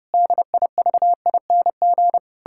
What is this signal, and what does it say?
Morse code, heard as a keyed tone, about 32 wpm, 700 Hz. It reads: DIVING
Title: Pangram Challenge v2 - 30wpm